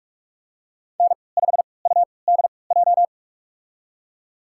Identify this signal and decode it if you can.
Morse code, heard as a keyed tone, about 46 wpm, spaced as for 25 wpm, 700 Hz. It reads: N5UDJ
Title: USA Call Signs - 5 Characters - 3x Character Spacing - 45wpm